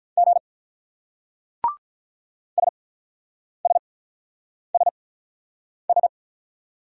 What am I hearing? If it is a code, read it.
Morse code, heard as a keyed tone, about 44 wpm, spaced as for 9 wpm, 700 Hz. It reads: G SSSS